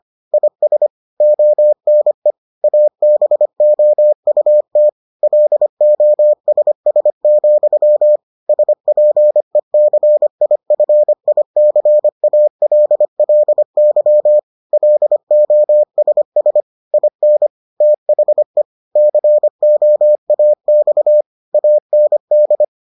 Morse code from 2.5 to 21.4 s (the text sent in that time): ABOUT LOSS, SPECIFICALLY LOSS IN THE COAX